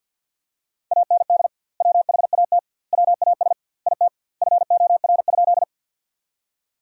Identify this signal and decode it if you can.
Morse code, heard as a keyed tone, about 50 wpm, 700 Hz. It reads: AND WHAT WAS IT FOR?
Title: 2x Word-Spacing - Sentences from Top 100 Words - 50wpm